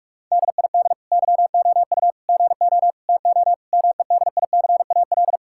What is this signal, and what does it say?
Morse code, heard as a keyed tone, about 46 wpm, 700 Hz. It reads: DID YOU GO TO MEDICAL
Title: Sentences from Top 1000 Words - Review Part 5 - 45wpm